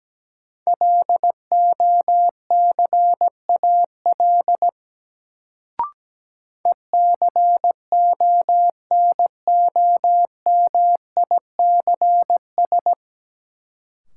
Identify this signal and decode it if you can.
Morse code, heard as a keyed tone, about 17 wpm, 700 Hz. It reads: LOCAL ECONOMICS
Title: Sentences from Top 1000 Words - Review Part 3 - New Word Drill - 17wpm